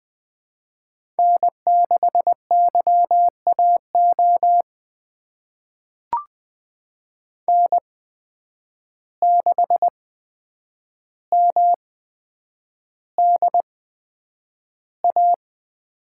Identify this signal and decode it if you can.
Morse code, heard as a keyed tone, about 21 wpm, spaced as for 9 wpm, 700 Hz. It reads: N6YAO N6MDA
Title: USA Call Signs - 5 Characters - 8x Character Spacing - 20wpm